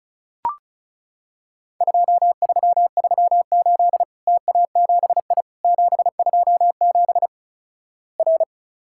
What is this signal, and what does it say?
Morse code, heard as a keyed tone, about 36 wpm, 700 Hz. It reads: 2338 TA7I 727 R
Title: Worked All Europe DX Contest - QTC Exchange - 35wpm